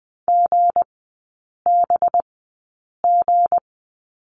Morse code, heard as a keyed tone, about 20 wpm, spaced as for 8 wpm, 700 Hz. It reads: GBG
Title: 191 Lesson 31e - Letter v - Three Characters 20wpm